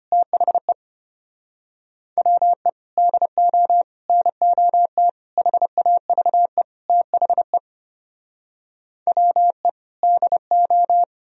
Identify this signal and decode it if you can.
Morse code, heard as a keyed tone, about 33 wpm, 700 Hz. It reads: THE WE DO NOT HAVE THE WE DO